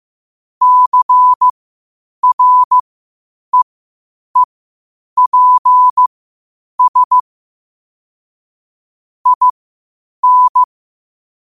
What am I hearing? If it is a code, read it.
Morse code, heard as a keyed tone, about 15 wpm, spaced as for 8 wpm, 1000 Hz. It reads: CREEPS IN
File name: qotdfiles / 8wpm